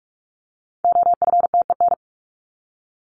Morse code, heard as a keyed tone, about 44 wpm, 700 Hz. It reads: OFTEN
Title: Top 500 Words 45wpm